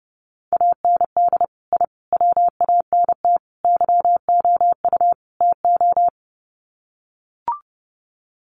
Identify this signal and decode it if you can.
Morse code, heard as a keyed tone, about 30 wpm, 700 Hz. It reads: AND I WANT YOU TO
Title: Sets of 5 Words Encore 30wpm